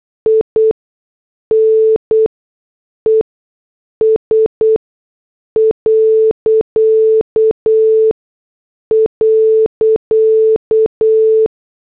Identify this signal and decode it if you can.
Morse code, heard as a keyed tone, about 8 wpm, spaced as for 6 wpm, 440 Hz. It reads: INES..